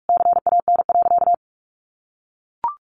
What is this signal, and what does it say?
Morse code, heard as a keyed tone, about 44 wpm, 700 Hz. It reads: CAN.
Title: Sentences from Top 1100 Words 45wpm